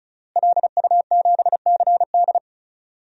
Morse code, heard as a keyed tone, about 36 wpm, 700 Hz. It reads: LU7CD
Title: DXCC-100 Call Signs - 5 Characters - Repeated 3x - 35wpm